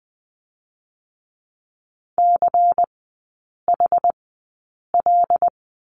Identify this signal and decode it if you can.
Morse code, heard as a keyed tone, about 20 wpm, spaced as for 8 wpm, 690 Hz. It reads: CHL
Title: 166 Lesson 27e - Number 7 - Three Characters 20wpm